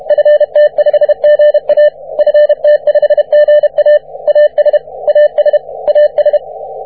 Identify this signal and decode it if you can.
Morse code, heard as a keyed tone, about 32 wpm, 590 Hz. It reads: FT5GA FT5GA AS AS AS
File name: wp / ft5ga 17m cw